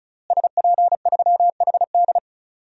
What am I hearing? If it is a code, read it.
Morse code, heard as a keyed tone, about 36 wpm, 700 Hz. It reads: SP3HD
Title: DXCC-100 Call Signs - 5 Characters - Repeated 3x - 35wpm